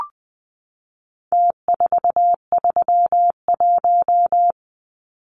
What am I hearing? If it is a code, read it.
Morse code, heard as a keyed tone, about 20 wpm, 700 Hz. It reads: T431
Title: Numbers - Contest Style - 4-Digit 20wpm